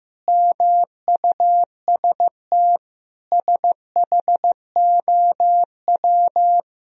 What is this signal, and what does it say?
Morse code, heard as a keyed tone, about 15 wpm, 700 Hz. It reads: MUST SHOW
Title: Sentences from Top 1000 Words - Review Part 4 - 15wpm